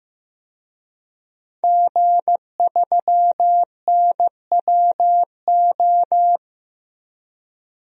Morse code, heard as a keyed tone, about 16 wpm, 700 Hz. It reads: G3NWO
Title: DXCC-100 Call Signs - 5 Characters - 15wpm